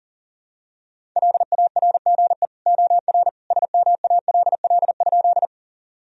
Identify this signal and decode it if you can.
Morse code, heard as a keyed tone, about 40 wpm, 700 Hz. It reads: LARGE OR SMALL?
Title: Sentences from Top 1000 Words - Review Part 1 - 40wpm